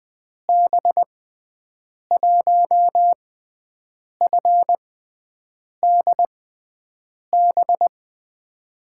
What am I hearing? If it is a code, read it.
Morse code, heard as a keyed tone, about 20 wpm, spaced as for 7 wpm, 700 Hz. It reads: B1FDB
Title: USA Call Signs - 6 Characters - 6x Character Spacing - 20wpm